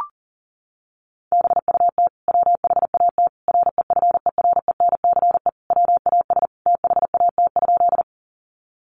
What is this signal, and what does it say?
Morse code, heard as a keyed tone, about 37 wpm, 700 Hz. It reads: BUT WHAT REFERENCE WAS THAT?
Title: Sentences from Top 1200 Words 40wpm